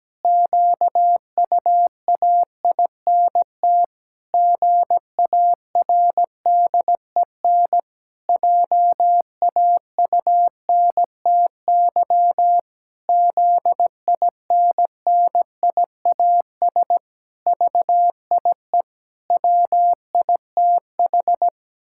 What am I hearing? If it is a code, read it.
Morse code, heard as a keyed tone, about 17 wpm, 700 Hz. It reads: QUAINT GARDEN JAUNTY ZINNIAS VIE WITH
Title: Pangram Challenge v2 - 17wpm